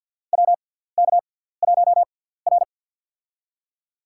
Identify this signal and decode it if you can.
Morse code, heard as a keyed tone, about 53 wpm, spaced as for 17 wpm, 700 Hz. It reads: WK1R
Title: USA Call Signs - 6x Character Spacing - 50wpm